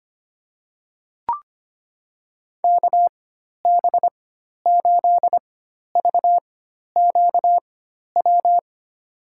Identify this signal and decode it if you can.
Morse code, heard as a keyed tone, about 25 wpm, spaced as for 12 wpm, 700 Hz. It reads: KB8VQW
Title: USA Call Signs - 6 Characters - 4x Character Spacing - 25wpm